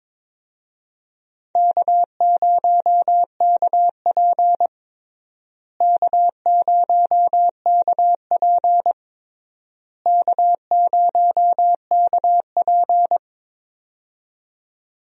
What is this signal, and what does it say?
Morse code, heard as a keyed tone, about 22 wpm, 700 Hz. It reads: K0KP K0KP K0KP
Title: USA Call Signs - 4 Characters - Repeated 3x - 22wpm